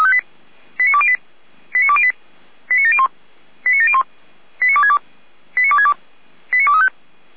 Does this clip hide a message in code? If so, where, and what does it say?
Yes: five-tone selective call (ZVEI) from 0.8 to 6.9 s, heard as quick runs of five tones, tones of about 70 ms. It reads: 89208 89208 89701 89701 89262 89262 89235